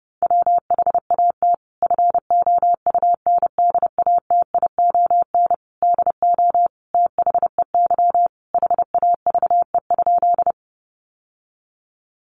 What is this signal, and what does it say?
Morse code, heard as a keyed tone, about 28 wpm, 700 Hz. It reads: WHAT FOUNDATION DO THEY HAVE?
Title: Sentences from Top 1200 Words 30wpm